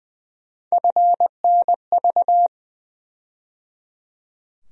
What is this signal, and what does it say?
Morse code, heard as a keyed tone, about 20 wpm, 700 Hz. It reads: FNV
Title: USA Call Sign Postfixes - 3 Letters 20wpm